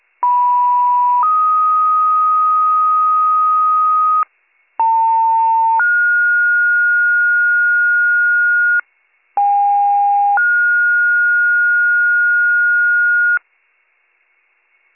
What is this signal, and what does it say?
Two-tone page, heard as a one-second tone, then a three-second tone, about 980 then 1250 Hz, about 900 then 1430 Hz, about 810 then 1430 Hz, with noise. It is heard from 0.2 to 13.4 s.